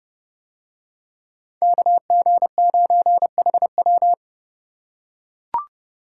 Morse code, heard as a keyed tone, about 29 wpm, 700 Hz. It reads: KG9HW